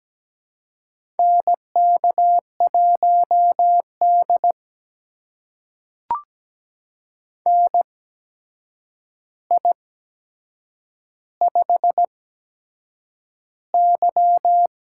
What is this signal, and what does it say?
Morse code, heard as a keyed tone, about 17 wpm, spaced as for 7 wpm, 700 Hz. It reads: NK1D NI5Y
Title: USA Call Signs - 4 Characters - 8x Character Spacing - 17wpm